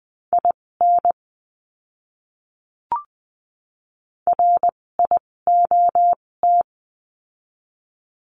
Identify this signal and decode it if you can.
Morse code, heard as a keyed tone, about 20 wpm, spaced as for 16 wpm, 700 Hz. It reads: IN RIOT
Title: 256 Lesson 42a - Words 12-20wpm